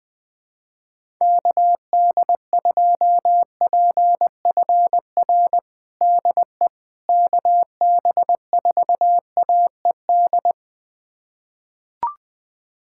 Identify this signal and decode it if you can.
Morse code, heard as a keyed tone, about 21 wpm, 700 Hz. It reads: KD2PFR DE KB4AED
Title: USA Callsign DE Callsign - 6 Characters - 20wpm